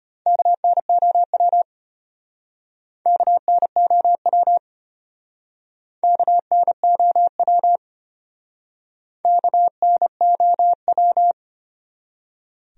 Morse code, heard as a keyed tone, about 33 wpm, 700 Hz. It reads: KNOW KNOW KNOW KNOW